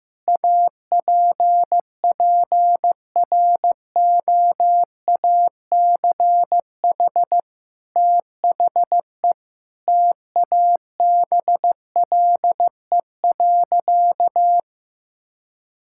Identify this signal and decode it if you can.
Morse code, heard as a keyed tone, about 15 wpm, 700 Hz. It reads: APPROACH THE TABLE.
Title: Sentences from Top 1000 Words - Review Part 6 - 15wpm